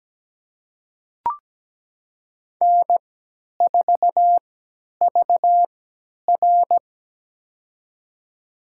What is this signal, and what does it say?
Morse code, heard as a keyed tone, about 17 wpm, spaced as for 10 wpm, 700 Hz. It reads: N4VR
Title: USA Call Signs - 4 Characters - 3x Character Spacing - 17wpm